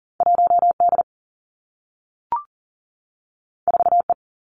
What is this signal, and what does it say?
Morse code, heard as a keyed tone, about 41 wpm, 700 Hz. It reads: 1 D 4 E
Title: Two Challenging Letter-Number Combinations 40wpm